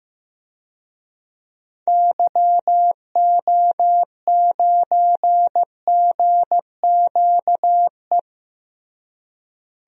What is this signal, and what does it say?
Morse code, heard as a keyed tone, about 15 wpm, 700 Hz. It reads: YO9GQE
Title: DXCC-100 Call Signs - 6 Characters - Repeated 2x - 15wpm